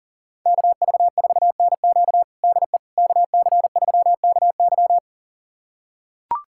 Morse code, heard as a keyed tone, about 39 wpm, 700 Hz. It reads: KV4NQ DE KC3KY
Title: USA Callsign DE Callsign 40wpm